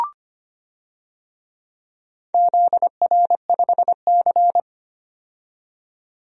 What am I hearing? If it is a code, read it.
Morse code, heard as a keyed tone, about 25 wpm, 700 Hz. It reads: ZR5C
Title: DXCC-100 Call Signs - 4 Characters - 25wpm